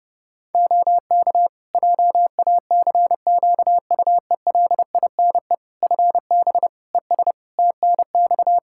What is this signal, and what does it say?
Morse code, heard as a keyed tone, about 32 wpm, 700 Hz. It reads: OK JACQUELINE FB ES TNX
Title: K1USN SST QSO Practice 30wpm